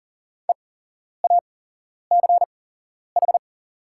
Morse code, heard as a keyed tone, about 40 wpm, spaced as for 11 wpm, 690 Hz. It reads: EACH
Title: 8x Character Spacing - Top 200 Words - 40wpm